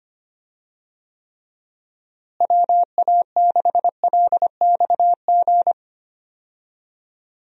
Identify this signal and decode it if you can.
Morse code, heard as a keyed tone, about 25 wpm, 700 Hz. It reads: WA6LXG
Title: USA Call Signs - 6 Characters - Repeated 3x - 25wpm